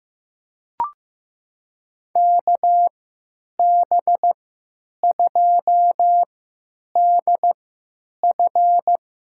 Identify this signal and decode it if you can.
Morse code, heard as a keyed tone, about 15 wpm, spaced as for 9 wpm, 700 Hz. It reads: KB2DF
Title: USA Call Signs - 3x Character Spacing - 15wpm